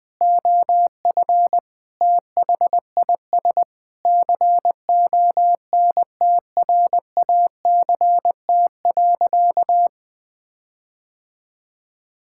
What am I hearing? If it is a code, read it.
Morse code, heard as a keyed tone, about 21 wpm, 700 Hz. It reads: OF THIS CONTRACT.
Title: Sentences with 9-Word Phrases - Spoken Beginning - 20wpm